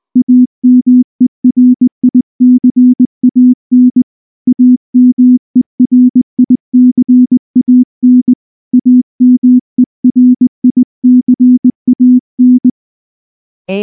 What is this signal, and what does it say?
Morse code, heard as a keyed tone, about 19 wpm, 250 Hz. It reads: AMERICAN AMERICAN AMERICAN
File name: morse / morse-training-20WPM-7